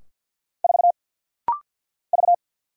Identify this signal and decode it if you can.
Morse code, heard as a keyed tone, about 49 wpm, 700 Hz. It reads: 4 V
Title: Single Letter-Number - Rapid-Fire 50wpm